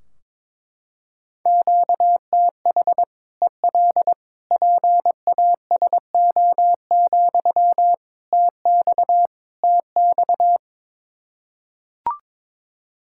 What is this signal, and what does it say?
Morse code, heard as a keyed tone, about 21 wpm, 700 Hz. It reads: QTH EL PASO, TX TX
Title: Practice QSO - Exchange 1 - Beginning - 22wpm